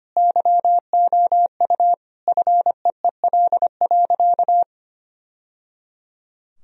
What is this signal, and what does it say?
Morse code, heard as a keyed tone, about 25 wpm, 700 Hz. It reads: YOU FEEL.
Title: Sentences from Top 1000 Words - Review Part 1 - 25wpm